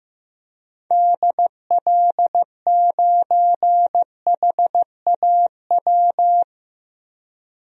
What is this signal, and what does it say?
Morse code, heard as a keyed tone, about 14 wpm, 700 Hz. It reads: DL9HAW
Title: DXCC-100 Call Signs - 6 Characters - Repeated 3x - 15wpm